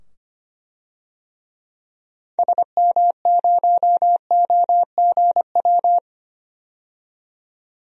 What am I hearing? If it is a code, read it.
Morse code, heard as a keyed tone, about 25 wpm, 700 Hz. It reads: SM0OGW
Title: DXCC-100 Call Signs - 6 Characters - 25wpm